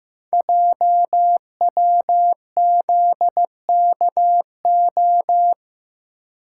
Morse code, heard as a keyed tone, about 15 wpm, 700 Hz. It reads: JWZKO
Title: Code Groups - 5 Letters - 15wpm